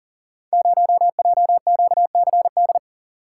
Morse code, heard as a keyed tone, about 39 wpm, 700 Hz. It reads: 0JQCD
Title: Code Groups - 5 Letter:Numbers - 40wpm